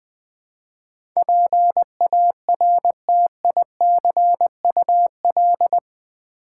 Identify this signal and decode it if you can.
Morse code, heard as a keyed tone, about 20 wpm, 700 Hz. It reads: PARTICUL
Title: Build A Word - 4 Letters At A Time - 10-Letter Words - 20wpm